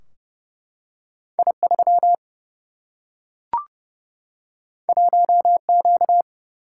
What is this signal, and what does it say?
Morse code, heard as a keyed tone, about 30 wpm, 700 Hz. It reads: I 3 1 Q
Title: Single-Letter Single-Number 30wpm